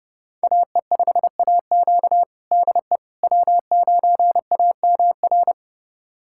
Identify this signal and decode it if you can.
Morse code, heard as a keyed tone, about 32 wpm, 700 Hz. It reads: AE5AQ DE W9AMR
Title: USA Callsign DE Callsign - 5 Characters - 30wpm